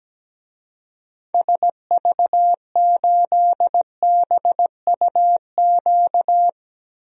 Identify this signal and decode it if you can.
Morse code, heard as a keyed tone, about 17 wpm, 700 Hz. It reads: SV8BUQ
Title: DXCC-100 Call Signs - 6 Characters - 17wpm